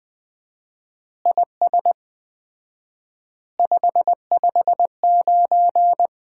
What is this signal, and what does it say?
Morse code, heard as a keyed tone, about 21 wpm, 700 Hz. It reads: IS 559
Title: Common QSO Phrases - 4x Word-Spacing - 20wpm